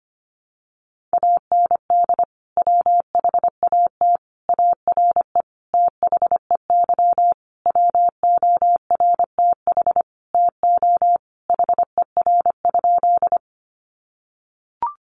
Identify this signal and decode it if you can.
Morse code, heard as a keyed tone, about 25 wpm, 700 Hz. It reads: AND WHAT ARE THEY WORTH TO HER?
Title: Sentences from Top 800 Words 25wpm